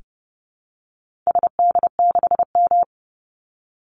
Morse code, heard as a keyed tone, about 32 wpm, 700 Hz. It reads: SD6M
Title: Call Signs - 30wpm